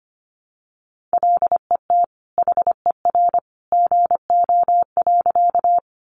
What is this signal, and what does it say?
Morse code, heard as a keyed tone, about 25 wpm, 700 Hz. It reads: LET HER GO.
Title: Sentences from Top 300 Words 25wpm